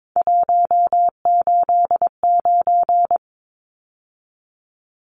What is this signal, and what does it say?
Morse code, heard as a keyed tone, about 22 wpm, 700 Hz. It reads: 189
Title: Numbers - 3-Digit 22wpm